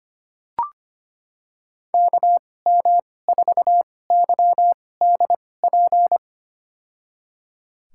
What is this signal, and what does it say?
Morse code, heard as a keyed tone, about 25 wpm, spaced as for 18 wpm, 700 Hz. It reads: KM4YDP
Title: USA Call Signs - 2x Character Spacing - 25wpm